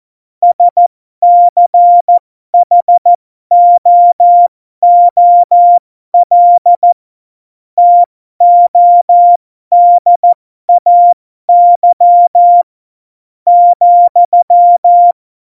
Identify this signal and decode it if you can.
Morse code, heard as a keyed tone, about 14 wpm, spaced as for 12 wpm, 700 Hz. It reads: SCHOOL TODAY ,
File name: Morsekurs / Witze